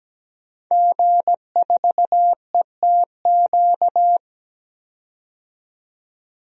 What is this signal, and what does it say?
Morse code, heard as a keyed tone, about 17 wpm, 700 Hz. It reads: G4ETQ
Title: DXCC-100 Call Signs - 5 Characters - 17wpm